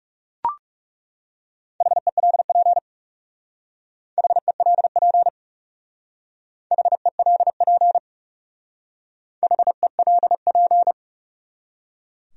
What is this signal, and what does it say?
Morse code, heard as a keyed tone, about 36 wpm, 700 Hz. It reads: HELP HELP HELP HELP
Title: Speed-Racing Top 200 Words 30wpm